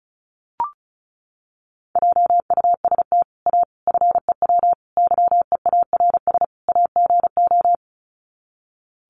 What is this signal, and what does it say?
Morse code, heard as a keyed tone, about 34 wpm, 700 Hz. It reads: JUST A FEW YEARS AGO
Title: Sets of 5 Words Encore 35wpm